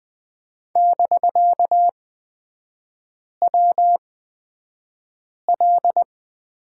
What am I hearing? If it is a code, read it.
Morse code, heard as a keyed tone, about 20 wpm, spaced as for 5 wpm, 700 Hz. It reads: <BK>WL
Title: 250 Lesson 40e - BK - Three Characters 20wpm